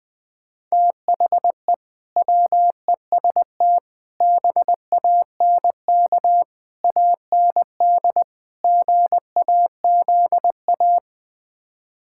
Morse code, heard as a keyed tone, about 20 wpm, 700 Hz. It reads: THE WEST BANK AND GAZA
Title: Build A Phrase - 5-Word Phrases - 20wpm